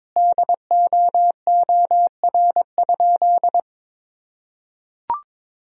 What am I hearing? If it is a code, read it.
Morse code, heard as a keyed tone, about 22 wpm, 700 Hz. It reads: DOOR?
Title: Sentences from Top 1000 Words - Review Part 2 - 22wpm